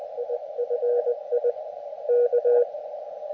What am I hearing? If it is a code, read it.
Morse code, heard as a keyed tone, about 19 wpm, 480 Hz, with radio noise. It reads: IFI K
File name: Expedition / SWL UT5NM 20.12.2012 VANGUARD UA1CEG p IT9IFI